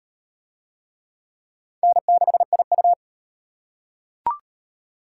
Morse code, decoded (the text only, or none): N6IU